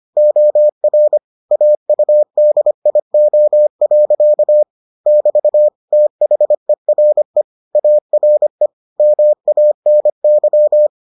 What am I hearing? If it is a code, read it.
Morse code, heard as a keyed tone, about 25 wpm, 600 Hz. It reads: OR AUDIO. <BT> THERE ARE MANY